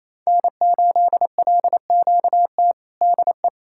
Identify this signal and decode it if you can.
Morse code, heard as a keyed tone, about 28 wpm, 700 Hz. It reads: N8LQT DE
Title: K1USN SST QSO Practice 28wpm